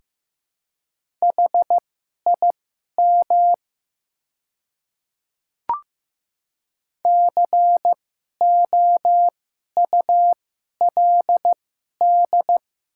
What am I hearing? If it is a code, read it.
Morse code, heard as a keyed tone, about 15 wpm, spaced as for 11 wpm, 700 Hz. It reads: HIM COULD
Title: 2x Character Spacing - Top 300 Words - 15wpm